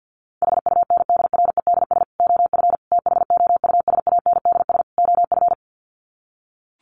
Morse code, heard as a keyed tone, about 50 wpm, 700 Hz. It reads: HUNDREDS OF THOUSANDS OF
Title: Sets of 4 Words 50wpm